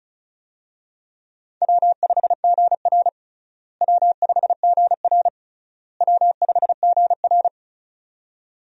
Morse code, heard as a keyed tone, about 36 wpm, 700 Hz. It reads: W5GR W5GR W5GR
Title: USA Call Signs - 4 Characters - Repeated 3x - 35wpm